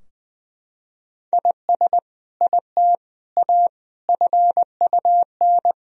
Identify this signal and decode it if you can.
Morse code, heard as a keyed tone, about 19 wpm, 700 Hz. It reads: IS IT A FUN
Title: Sentences from Top 1000 Words - Review Part 4 - 20wpm